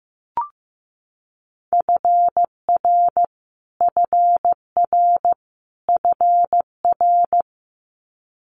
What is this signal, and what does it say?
Morse code, heard as a keyed tone, about 15 wpm, 700 Hz. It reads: FR FR FR
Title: Multi-letter Phonemes Repeated 3x 15wpm